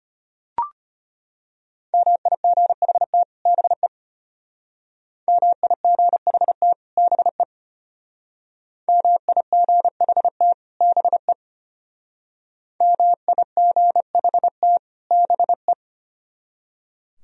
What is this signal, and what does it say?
Morse code, heard as a keyed tone, about 32 wpm, 700 Hz. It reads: MIGHT BE MIGHT BE MIGHT BE MIGHT BE